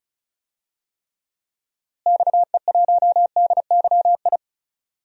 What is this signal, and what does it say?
Morse code, heard as a keyed tone, about 36 wpm, 700 Hz. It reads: XE1DYI
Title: DXCC-100 Call Signs - 6 Characters - Repeated 3x - 35wpm